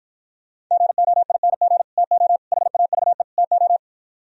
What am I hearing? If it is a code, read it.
Morse code, heard as a keyed tone, about 53 wpm, 700 Hz. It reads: GOING TO HAVE TO